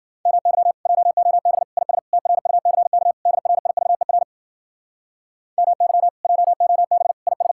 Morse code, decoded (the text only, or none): MY JOB IS TRUCK DRIVER MY JOB IS